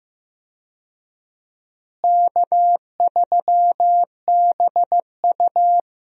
Morse code, heard as a keyed tone, about 15 wpm, 700 Hz. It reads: K3BU
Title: USA Call Signs - 4 Characters - Repeated 3x - 15wpm